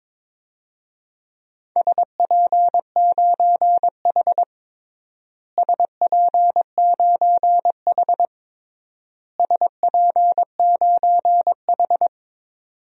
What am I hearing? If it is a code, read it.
Morse code, heard as a keyed tone, about 22 wpm, 700 Hz. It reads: SP9H SP9H SP9H
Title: DXCC-100 Call Signs - 4 Characters - Repeated 3x - 22wpm